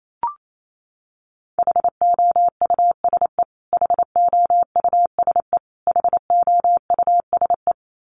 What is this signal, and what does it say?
Morse code, heard as a keyed tone, about 28 wpm, 700 Hz. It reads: HOUSE HOUSE HOUSE
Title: Top 200 Words Repeated 28wpm